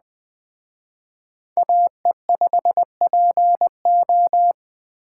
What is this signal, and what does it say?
Morse code, heard as a keyed tone, about 20 wpm, 700 Hz. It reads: AE5PO
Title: Speed Racing - USA 2x2 Call Signs 17wpm